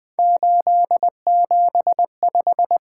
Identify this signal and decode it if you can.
Morse code, heard as a keyed tone, about 20 wpm, 700 Hz. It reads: 875
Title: Three Letter-Number Combinations - 20wpm